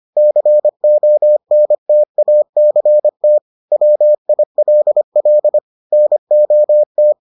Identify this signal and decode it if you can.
Morse code, heard as a keyed tone, about 25 wpm, 600 Hz. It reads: CONTACT WILL NOT